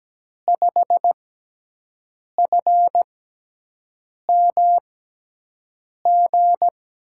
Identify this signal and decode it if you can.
Morse code, heard as a keyed tone, about 17 wpm, spaced as for 6 wpm, 700 Hz. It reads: 5FMG
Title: USA Call Signs - 6 Characters - 6x Character Spacing - 17wpm